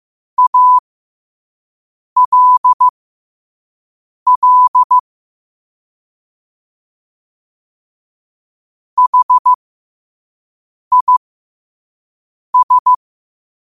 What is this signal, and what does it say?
Morse code, heard as a keyed tone, about 15 wpm, spaced as for 5 wpm, 1000 Hz. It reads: ALL HIS
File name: qotdfiles / 5wpm